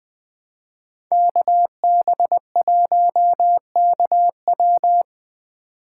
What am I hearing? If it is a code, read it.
Morse code, heard as a keyed tone, about 21 wpm, 700 Hz. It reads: KB1KW